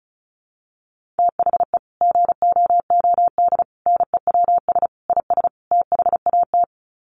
Morse code, heard as a keyed tone, about 34 wpm, 700 Hz. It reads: THE GOOD NEWS IS THAT
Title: Sets of 5 Words 35wpm